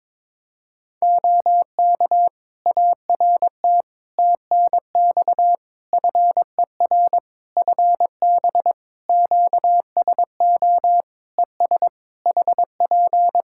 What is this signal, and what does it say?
Morse code, heard as a keyed tone, about 21 wpm, 700 Hz. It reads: OK ART TNX FER FB QSO ES HP
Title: Practice QSO - Exchange 4 - Ending - 22wpm